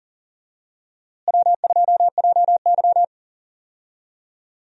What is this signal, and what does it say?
Morse code, heard as a keyed tone, about 37 wpm, 700 Hz. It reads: W2JY
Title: USA Call Signs - 4 Characters - Repeated 3x - 40wpm